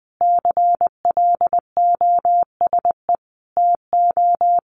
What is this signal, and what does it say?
Morse code, heard as a keyed tone, about 20 wpm, 700 Hz. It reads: CLOSE TO
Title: Sentences from Top 900 Words 20wpm